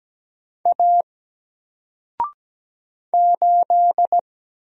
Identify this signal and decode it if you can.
Morse code, heard as a keyed tone, about 17 wpm, 700 Hz. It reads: A 8
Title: Single Character (Letter:Number:Punctuation:Prosign) Alt 17wpm